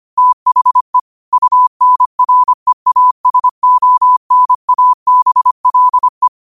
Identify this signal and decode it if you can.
Morse code, heard as a keyed tone, about 25 wpm, 1000 Hz. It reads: THE UNREASONABLE
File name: qotdfiles / 25wpm